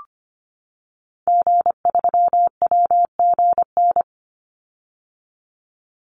Morse code, heard as a keyed tone, about 25 wpm, 700 Hz. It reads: G3WGN
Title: Call Signs - 25wpm